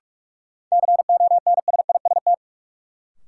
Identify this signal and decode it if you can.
Morse code, heard as a keyed tone, about 45 wpm, 700 Hz. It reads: CONSIST
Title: Sentences from Top 1400 Words - New Word Drill Alt - 45wpm